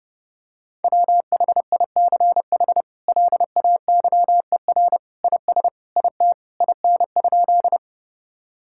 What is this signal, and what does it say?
Morse code, heard as a keyed tone, about 30 wpm, 700 Hz. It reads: WHICH LAYER IS IT IN?
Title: Sentences from Top 1600 Words 30wpm